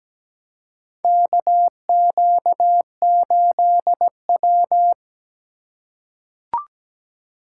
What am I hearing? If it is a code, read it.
Morse code, heard as a keyed tone, about 17 wpm, 700 Hz. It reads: KQ8W
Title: USA Call Signs - 4 Characters - 4x Character Spacing - 17wpm